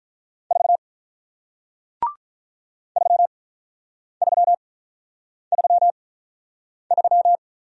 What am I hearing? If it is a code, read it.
Morse code, heard as a keyed tone, about 48 wpm, spaced as for 9 wpm, 700 Hz. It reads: 4 3333